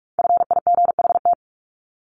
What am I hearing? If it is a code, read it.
Morse code, heard as a keyed tone, about 44 wpm, 700 Hz. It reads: FIGHT
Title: Top 1501-1750 Words 45wpm